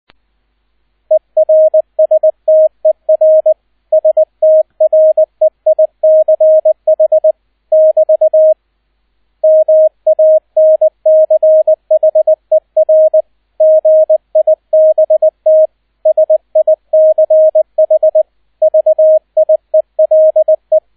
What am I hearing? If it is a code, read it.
Morse code, heard as a keyed tone, about 20 wpm, 630 Hz, with radio noise. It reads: ERSTER STREICH <BT> MANCHER GIBT SICH VIELE
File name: Erster Streich 20wpm